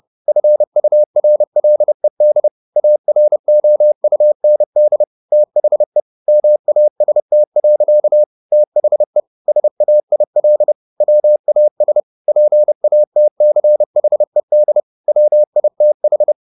Morse code, heard as a keyed tone, about 30 wpm, 600 Hz. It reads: FURLED AROUND THE MAST. THE SAIL WAS PATCHED WITH
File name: the old man and the sea / the old man and the sea 1 30wpm